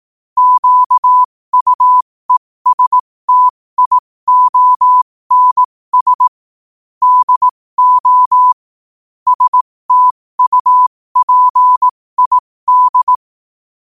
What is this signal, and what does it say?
Morse code, heard as a keyed tone, about 18 wpm, spaced as for 15 wpm, 1000 Hz. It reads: QUESTIONS DO STUPID
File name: qotdfiles / 15wpm